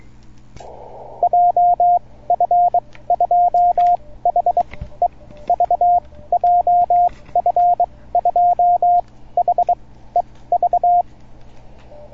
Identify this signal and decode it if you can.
Morse code, heard as a keyed tone, about 23 wpm, spaced as for 16 wpm, 710 Hz, with radio noise. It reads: JF2HEV JF2HEV